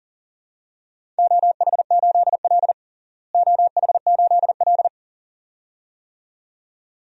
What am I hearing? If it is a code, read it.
Morse code, heard as a keyed tone, about 40 wpm, 700 Hz. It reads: OH8L OH8L
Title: DXCC-100 Call Signs - 4 Characters - Repeated 2x - 40wpm